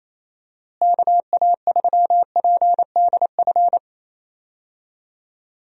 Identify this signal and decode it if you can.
Morse code, heard as a keyed tone, about 28 wpm, 700 Hz. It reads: KA3PDF
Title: USA Call Signs - 6 Characters - Repeated 3x - 28wpm